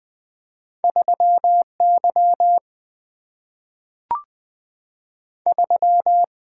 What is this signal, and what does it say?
Morse code, heard as a keyed tone, about 20 wpm, 700 Hz. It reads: 3 Y 3
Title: Two Letter-Number Combinations - 20wpm